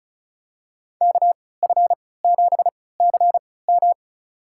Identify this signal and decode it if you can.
Morse code, heard as a keyed tone, about 36 wpm, spaced as for 20 wpm, 700 Hz. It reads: KF7CM
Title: USA Call Signs - 5 Characters - 3x Character Spacing - 35wpm